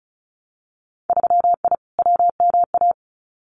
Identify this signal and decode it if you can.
Morse code, heard as a keyed tone, about 36 wpm, 700 Hz. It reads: 3I WMA
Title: Winter Field Day Exchanges 35wpm